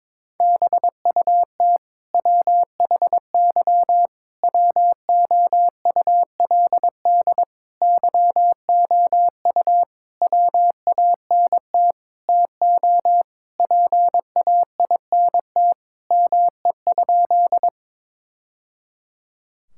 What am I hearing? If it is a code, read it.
Morse code, heard as a keyed tone, about 22 wpm, 700 Hz. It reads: BUT WHY WOULD YOU WANT TO PAINT ME?
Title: Sentences from Top 1000 Words - Review Part 6 - 22wpm